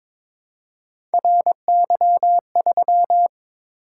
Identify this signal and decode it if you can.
Morse code, heard as a keyed tone, about 22 wpm, 700 Hz. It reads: RY3
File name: practice / 22wpm / Three Letter-Number Combinations - 22wpm